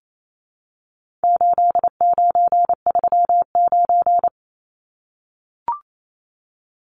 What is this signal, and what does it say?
Morse code, heard as a keyed tone, about 28 wpm, 700 Hz. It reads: 8939
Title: Numbers - 4-Digit 28wpm